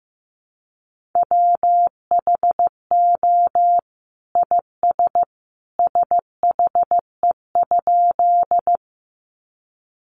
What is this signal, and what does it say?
Morse code, heard as a keyed tone, about 15 wpm, 700 Hz. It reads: WHO IS SHE?
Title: Sentences from Top 100 Words 15wpm